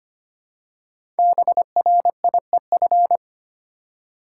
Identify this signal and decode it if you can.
Morse code, heard as a keyed tone, about 25 wpm, 700 Hz. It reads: BRIEF
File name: 5 Letter Words - First 3 Letters Given - 25wpm